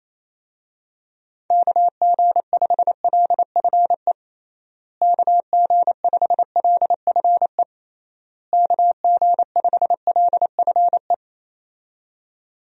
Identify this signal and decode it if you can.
Morse code, heard as a keyed tone, about 27 wpm, 700 Hz. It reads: KG5LFE KG5LFE KG5LFE
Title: USA Call Signs - 6 Characters - Repeated 3x - 28wpm